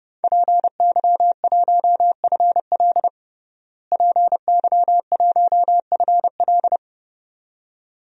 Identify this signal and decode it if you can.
Morse code, heard as a keyed tone, about 30 wpm, 700 Hz. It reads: PY1FL PY1FL
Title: DXCC-100 Call Signs - 5 Characters - Repeated 2x - 30wpm